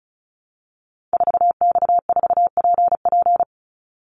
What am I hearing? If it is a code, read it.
Morse code, heard as a keyed tone, about 34 wpm, 700 Hz. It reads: 4X4PP
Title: Call Signs - 35wpm